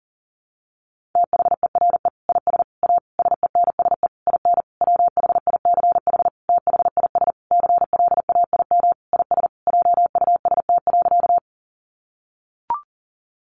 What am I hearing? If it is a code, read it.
Morse code, heard as a keyed tone, about 40 wpm, 700 Hz. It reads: THERE IS A SENSE IN WHICH THIS CLAIM IS JUST.
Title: Sentences from Top 1500 Words 40wpm